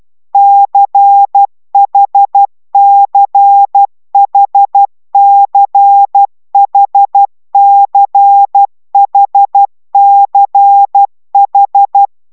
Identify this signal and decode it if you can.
Morse code, heard as a keyed tone, about 12 wpm, 800 Hz. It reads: CHCHCHCHCH